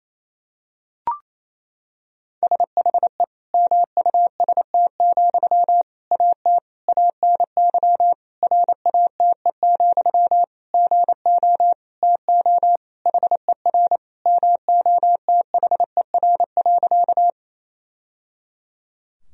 Morse code, heard as a keyed tone, about 28 wpm, 700 Hz. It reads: SHE MUST, AT ANY RATE, GO TO HER MOTHER.
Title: Sentences from Top 1000 Words - Review Part 4 - 28wpm